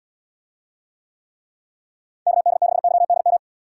Morse code, heard as a keyed tone, about 78 wpm, 700 Hz. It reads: KD6YDM